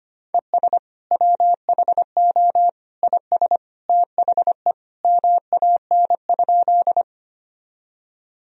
Morse code, heard as a keyed tone, about 25 wpm, 700 Hz. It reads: ES WHO IS THE MAN?
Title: Sentences from Top 200 Words and Abbreviations 25wpm